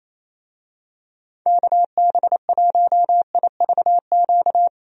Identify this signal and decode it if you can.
Morse code, heard as a keyed tone, about 28 wpm, 700 Hz. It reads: KB1IVQ
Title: USA Call Signs - 6 Characters - Repeated 3x - 28wpm